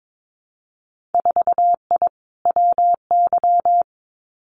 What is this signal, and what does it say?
Morse code, heard as a keyed tone, about 22 wpm, 700 Hz. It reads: 4I WY
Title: Winter Field Day Exchanges Encore 22wpm